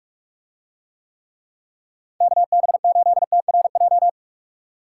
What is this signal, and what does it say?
Morse code, heard as a keyed tone, about 46 wpm, 700 Hz. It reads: KB8TRJ